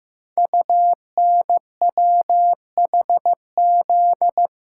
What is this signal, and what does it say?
Morse code, heard as a keyed tone, about 15 wpm, 700 Hz. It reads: UNWHZ
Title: Code Groups - 5 Letters - 15wpm